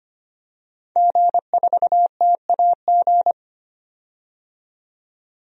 Morse code, heard as a keyed tone, about 25 wpm, 700 Hz. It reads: G4TAG
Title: DXCC-100 Call Signs - 5 Characters - 25wpm